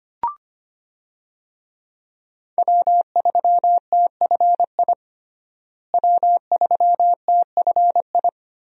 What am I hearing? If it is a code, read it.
Morse code, heard as a keyed tone, about 25 wpm, 700 Hz. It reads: W3TFI W3TFI
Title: USA Call Signs - 5 Characters - Repeated 3x - 25wpm